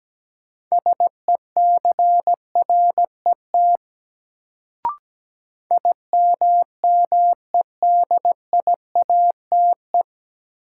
Morse code, heard as a keyed tone, about 17 wpm, 700 Hz. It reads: SECRET IMMEDIATE
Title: Sentences from Top 1200 Words - New Word Drill Alt - 17wpm